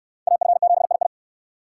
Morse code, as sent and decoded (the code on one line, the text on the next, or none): .- ..-. --... .. ..
AF7II